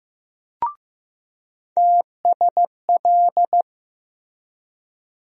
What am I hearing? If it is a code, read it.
Morse code, heard as a keyed tone, about 15 wpm, 700 Hz. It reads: TSL
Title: USA Call Sign Postfixes - 3 Letters 15wpm